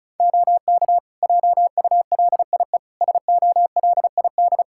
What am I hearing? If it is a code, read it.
Morse code, heard as a keyed tone, about 36 wpm, 700 Hz. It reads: OK JULIE SOLID
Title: Practice QSO - Exchange 3 - More Info - 35wpm